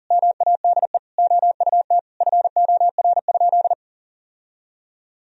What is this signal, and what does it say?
Morse code, heard as a keyed tone, about 40 wpm, 700 Hz. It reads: MADE OUT FOR?
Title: Sentences from Top 1000 Words - Review Part 3 - 40wpm